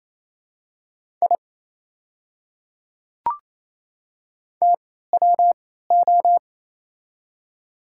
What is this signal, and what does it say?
Morse code, heard as a keyed tone, about 28 wpm, spaced as for 16 wpm, 690 Hz. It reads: I TWO
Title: 3x Character Spacing - Top 500 Words - 28wpm